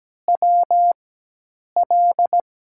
Morse code, heard as a keyed tone, about 17 wpm, spaced as for 8 wpm, 700 Hz. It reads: WL
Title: USA Call Signs - 5 Characters - 4x Character Spacing - 17wpm